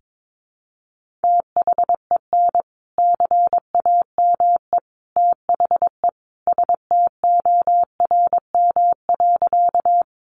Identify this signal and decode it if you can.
Morse code, heard as a keyed tone, about 22 wpm, 700 Hz. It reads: THEN CAME THE STORM.
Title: Sentences from Top 1500 Words 22wpm